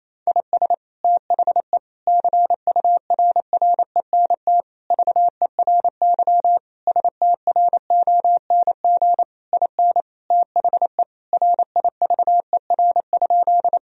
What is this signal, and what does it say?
Morse code, heard as a keyed tone, about 28 wpm, 700 Hz. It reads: IS THE CURRENT VERY STRONG IN THE RIVER?
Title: Sentences from Top 1000 Words - Review Part 4 - 28wpm